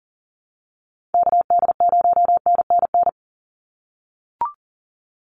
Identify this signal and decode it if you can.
Morse code, heard as a keyed tone, about 40 wpm, 700 Hz. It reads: KD0NNN
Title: USA Call Signs 40wpm